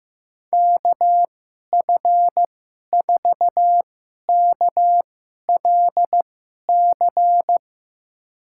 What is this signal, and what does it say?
Morse code, heard as a keyed tone, about 15 wpm, spaced as for 11 wpm, 700 Hz. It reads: KF4KLC
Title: USA Call Signs - 6 Characters - 2x Character Spacing - 15wpm